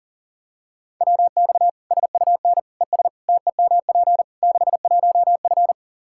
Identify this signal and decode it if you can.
Morse code, heard as a keyed tone, about 40 wpm, 700 Hz. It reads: WX SUN ES TEMP 61F
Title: Practice QSO - Exchange 2 - Info - 40wpm